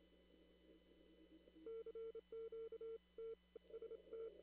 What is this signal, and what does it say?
Morse code, heard as a keyed tone, about 27 wpm, spaced as for 21 wpm, 450 Hz. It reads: CQ TST
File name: CW / kt5j wpx cw-op-k5pi